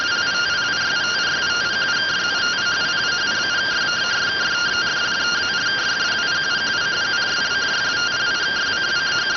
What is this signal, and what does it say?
Two-tone teleprinter signal, about 170 Hz shift.